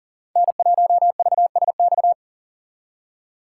Morse code, heard as a keyed tone, about 40 wpm, 700 Hz. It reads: N1VSX
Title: USA Call Signs - 5 Characters - 3x Character Spacing - 40wpm